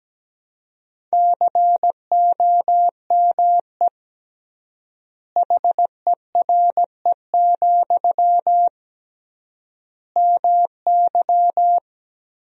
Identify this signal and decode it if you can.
Morse code, heard as a keyed tone, about 17 wpm, 700 Hz. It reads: COME HERE, MY
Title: 3x Word-Spacing - Sentences from Top 300 Words - 17wpm